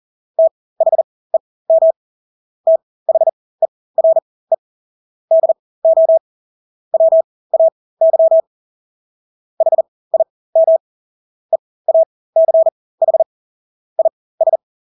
Morse code, heard as a keyed tone, about 40 wpm, spaced as for 20 wpm, 650 Hz. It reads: THEM THERE DO WAY HIM EACH IS
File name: Engelske ord 40-20 wpm